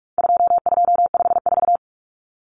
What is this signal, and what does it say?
Morse code, heard as a keyed tone, about 44 wpm, 700 Hz. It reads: 2254
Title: Common Cut Numbers - 4-Digit 45wpm